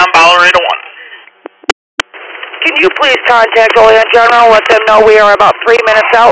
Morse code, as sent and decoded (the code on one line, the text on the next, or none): none